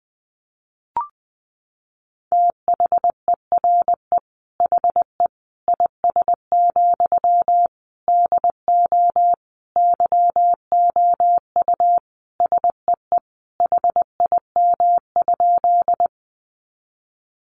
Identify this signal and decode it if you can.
Morse code, heard as a keyed tone, about 20 wpm, 700 Hz. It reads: THERE HE IS, DO YOU SEE HIM?
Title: Sentences from Top 100 Words 20wpm